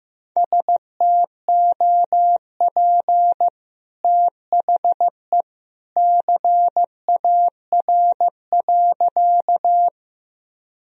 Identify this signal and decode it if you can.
Morse code, heard as a keyed tone, about 14 wpm, 700 Hz. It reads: STOP THE CAR.
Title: Sentences from Top 1000 Words - Review Part 1 - 15wpm